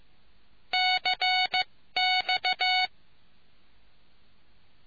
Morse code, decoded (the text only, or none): CX